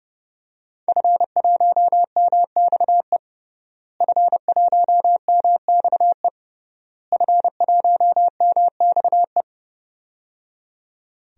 Morse code, heard as a keyed tone, about 30 wpm, 700 Hz. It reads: F1MXE F1MXE F1MXE
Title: DXCC-100 Call Signs - 5 Characters - Repeated 3x - 30wpm